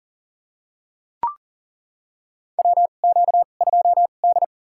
Morse code, heard as a keyed tone, about 42 wpm, spaced as for 29 wpm, 690 Hz. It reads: WQ2D
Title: USA Call Signs - 4 Characters - 2x Character Spacing - 40wpm